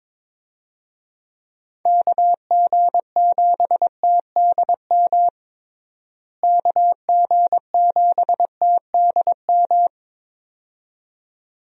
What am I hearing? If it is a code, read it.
Morse code, heard as a keyed tone, about 22 wpm, 700 Hz. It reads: KG7TDM KG7TDM
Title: USA Call Signs - 6 Characters - Repeated 2x - 22wpm